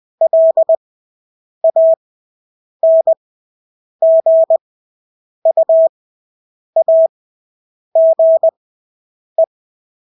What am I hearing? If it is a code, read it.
Morse code, heard as a keyed tone, about 20 wpm, spaced as for 8 wpm, 650 Hz. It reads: LANGUAGE
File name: Engelske ord 20/8 wpm